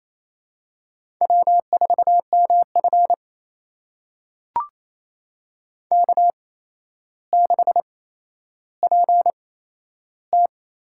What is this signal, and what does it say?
Morse code, heard as a keyed tone, about 28 wpm, spaced as for 12 wpm, 700 Hz. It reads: W4MF K6PT
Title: USA Call Signs - 4 Characters - 8x Character Spacing - 28wpm